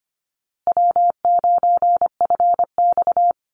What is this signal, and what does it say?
Morse code, heard as a keyed tone, about 25 wpm, 700 Hz. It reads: W9FX
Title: Easy Call Signs Repeated 3x - 25wpm